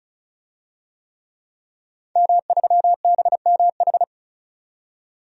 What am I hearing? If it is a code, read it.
Morse code, heard as a keyed tone, about 36 wpm, 700 Hz. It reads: M3BMH
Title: DXCC-100 Call Signs - 5 Characters - 35wpm